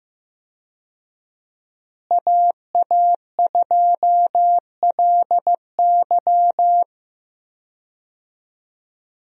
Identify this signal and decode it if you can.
Morse code, heard as a keyed tone, about 15 wpm, 700 Hz. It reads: AA2LY
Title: USA Call Signs - 5 Characters - Repeated 3x - 15wpm